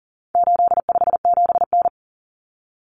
Morse code, heard as a keyed tone, about 40 wpm, 700 Hz. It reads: 857N
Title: Common Cut Numbers - 4-Digit 40wpm